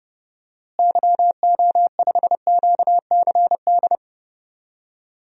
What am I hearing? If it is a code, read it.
Morse code, heard as a keyed tone, about 30 wpm, 700 Hz. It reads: YO5QCD
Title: DXCC-100 Call Signs - 6 Characters - Repeated 2x - 30wpm